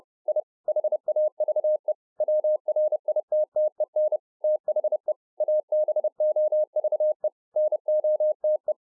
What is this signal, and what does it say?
Morse code, heard as a keyed tone, about 30 wpm, 610 Hz. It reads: I HAVE WRITTEN THE ABOVE NOTE